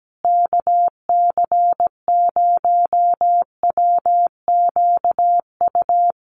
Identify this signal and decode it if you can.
Morse code, heard as a keyed tone, about 17 wpm, 700 Hz. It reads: KC0WQU
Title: USA Call Signs Repeated 2x 17wpm